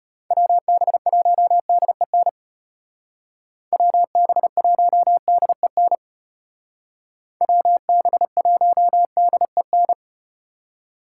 Morse code, heard as a keyed tone, about 33 wpm, 700 Hz. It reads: WB1DEN WB1DEN WB1DEN